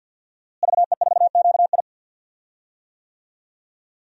Morse code, heard as a keyed tone, about 48 wpm, 700 Hz. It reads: VE4QI
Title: DXCC-100 Call Signs - 5 Characters - Repeated 3x - 50wpm